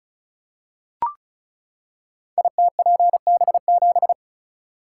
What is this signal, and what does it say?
Morse code, heard as a keyed tone, about 36 wpm, 700 Hz. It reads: ITPB7
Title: Code Groups - 5 Letter:Numbers - 35wpm